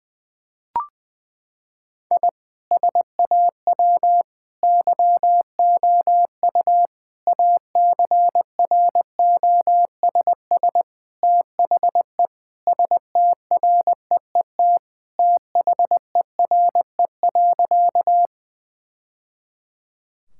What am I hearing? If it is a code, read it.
Morse code, heard as a keyed tone, about 20 wpm, 700 Hz. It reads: I SAW YOU ACROSS THE STREET THERE.
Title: Sentences from Top 1000 Words - Review Part 5 - 20wpm